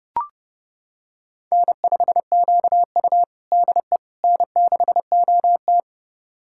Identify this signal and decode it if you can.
Morse code, heard as a keyed tone, about 29 wpm, 700 Hz. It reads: N5QU DE N6OT
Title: USA Callsign DE Callsign - 4 Characters - 30wpm